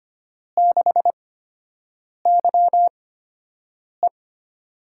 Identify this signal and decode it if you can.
Morse code, heard as a keyed tone, about 25 wpm, spaced as for 7 wpm, 700 Hz. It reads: 6YE
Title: USA Call Signs - 8x Character Spacing - 25wpm